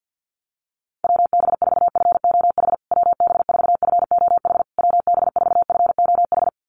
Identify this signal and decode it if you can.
Morse code, heard as a keyed tone, about 49 wpm, 700 Hz. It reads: WB4FOH WB4FOH WB4FOH
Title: USA Call Signs Repeated 3x 50wpm